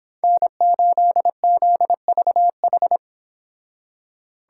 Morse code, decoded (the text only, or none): N8ZVH